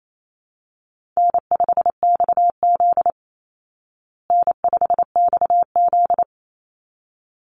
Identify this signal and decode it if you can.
Morse code, heard as a keyed tone, about 28 wpm, 700 Hz. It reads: N5XZ N5XZ
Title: Call Signs Repeated 3x - 28wpm